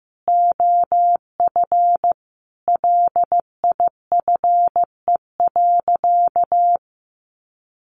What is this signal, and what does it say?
Morse code, heard as a keyed tone, about 15 wpm, 700 Hz. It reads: OF LIFE.
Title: Sentences from Top 800 Words 15wpm